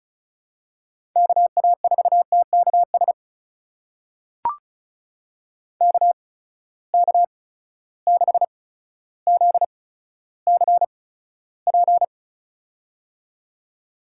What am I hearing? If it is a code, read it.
Morse code, heard as a keyed tone, about 36 wpm, spaced as for 15 wpm, 700 Hz. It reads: KA4TKS KK6ZCP
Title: USA Call Signs - 8x Character Spacing - 35wpm